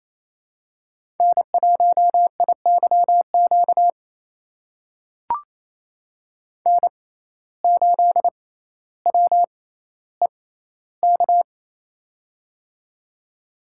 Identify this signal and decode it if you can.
Morse code, heard as a keyed tone, about 27 wpm, spaced as for 14 wpm, 700 Hz. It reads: N1IYQ N8WEK
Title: USA Call Signs - 5 Characters - 6x Character Spacing - 28wpm